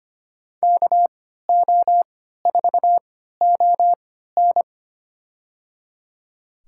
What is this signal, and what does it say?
Morse code, heard as a keyed tone, about 25 wpm, spaced as for 14 wpm, 700 Hz. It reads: KO4ON
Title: USA Call Signs - 3x Character Spacing - 25wpm